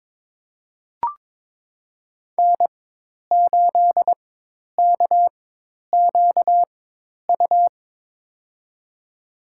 Morse code, heard as a keyed tone, about 23 wpm, spaced as for 10 wpm, 700 Hz. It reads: N8KQU
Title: USA Call Signs - 5 Characters - 4x Character Spacing - 22wpm